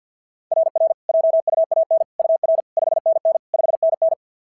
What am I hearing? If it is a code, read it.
Morse code, heard as a keyed tone, about 52 wpm, 690 Hz. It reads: RR JUAN UR 5NN 5NN
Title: SOTA QSO Practice 50wpm